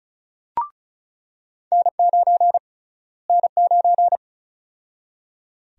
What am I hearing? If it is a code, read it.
Morse code, heard as a keyed tone, about 36 wpm, 700 Hz. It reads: N9 N9
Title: USA Call Sign Prefixes - Repeated 2x - 35wpm